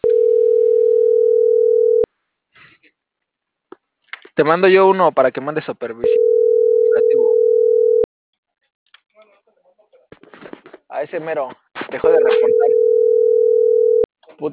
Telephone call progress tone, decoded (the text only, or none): ringback tone